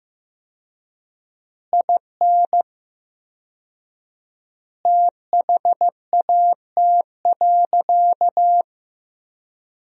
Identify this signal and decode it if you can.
Morse code, heard as a keyed tone, about 15 wpm, 700 Hz. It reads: IN THAT.
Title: 4x Word-Spacing - Sentences from Top 200 Words - 15wpm